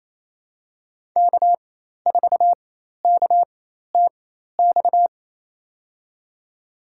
Morse code, heard as a keyed tone, about 28 wpm, spaced as for 13 wpm, 700 Hz. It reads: K4KTX
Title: USA Call Signs - 4x Character Spacing - 28wpm